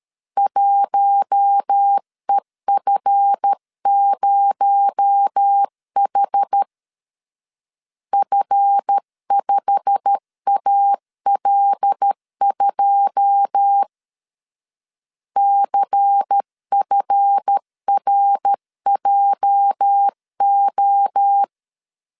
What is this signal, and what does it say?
Morse code, heard as a keyed tone, about 13 wpm, 800 Hz. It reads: 1EF0H F5AL2 CFRJO
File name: LEZIONI / 9 28 12wpm 2351